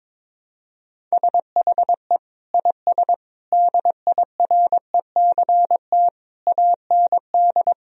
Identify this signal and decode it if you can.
Morse code, heard as a keyed tone, about 21 wpm, 700 Hz. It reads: SHE IS DIRECT AND
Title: Sentences from Top 1000 Words - Review Part 4 - 22wpm